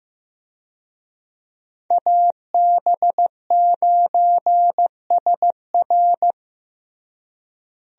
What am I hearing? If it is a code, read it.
Morse code, heard as a keyed tone, about 16 wpm, 700 Hz. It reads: AB9SR
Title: USA Call Signs - 5 Characters - Repeated 2x - 15wpm